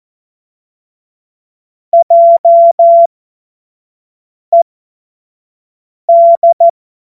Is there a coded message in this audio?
Yes: Morse code from 1.9 to 6.7 s, heard as a keyed tone, about 14 wpm, spaced as for 5 wpm, 680 Hz. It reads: JED